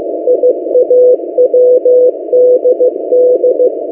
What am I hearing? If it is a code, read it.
Morse code, heard as a keyed tone, about 16 wpm, 500 Hz. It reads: IAWDD
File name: M08a- -1355utc-8096khz